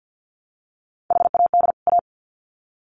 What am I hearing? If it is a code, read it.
Morse code, heard as a keyed tone, about 50 wpm, 700 Hz. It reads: HAD A
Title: Sets of 2 Words 50wpm